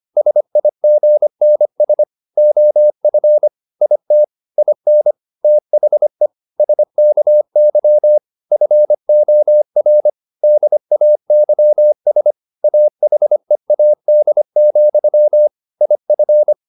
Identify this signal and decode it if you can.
Morse code, heard as a keyed tone, about 25 wpm, 600 Hz. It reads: SIGNS OF IT IN THE SKY FOR DAYS AHEAD, IF
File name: the old man and the sea part 1 25wpm